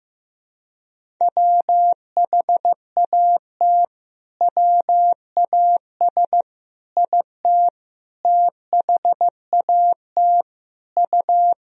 Morse code, heard as a keyed tone, about 15 wpm, 700 Hz. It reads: WHAT WAS IT THAT U
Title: Sentences from Top 300 Words and Abbreviations 15wpm